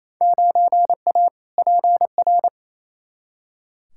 Morse code, heard as a keyed tone, about 28 wpm, 700 Hz. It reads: 9A PR
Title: Field Day Exchanges Encore - 28wpm